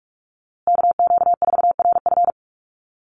Morse code, heard as a keyed tone, about 44 wpm, 700 Hz. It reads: KQ4RF
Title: USA Call Signs Repeated 3x 45wpm